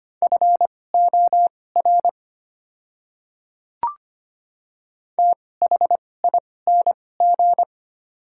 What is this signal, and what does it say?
Morse code, heard as a keyed tone, about 25 wpm, spaced as for 18 wpm, 700 Hz. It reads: FOR THING
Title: 2x Character Spacing - Top 100 Words - 25wpm